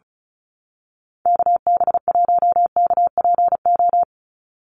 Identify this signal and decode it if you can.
Morse code, heard as a keyed tone, about 34 wpm, 700 Hz. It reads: KB1KPO
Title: USA Call Signs 35wpm